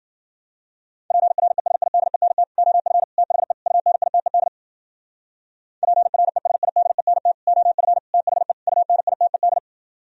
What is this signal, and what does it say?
Morse code, heard as a keyed tone, about 51 wpm, 700 Hz. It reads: PRESIDENT OF THE UNITED PRESIDENT OF THE UNITED